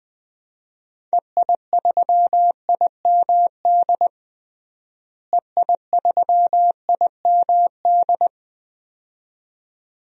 Morse code, heard as a keyed tone, about 20 wpm, 690 Hz. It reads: EI3IMD EI3IMD
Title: DXCC-100 Call Signs - 6 Characters - Repeated 2x - 20wpm